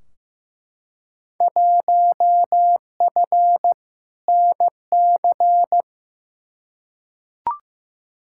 Morse code, heard as a keyed tone, about 15 wpm, 700 Hz. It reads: 1F NC
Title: Field Day Exchanges - 15wpm